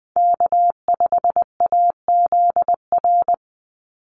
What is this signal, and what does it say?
Morse code, heard as a keyed tone, about 20 wpm, 700 Hz. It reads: K5AZR
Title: Call Signs - 20wpm